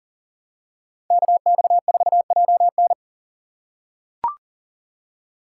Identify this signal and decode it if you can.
Morse code, heard as a keyed tone, about 40 wpm, 700 Hz. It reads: KX4JN
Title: USA Call Signs - 4x Character Spacing - 40wpm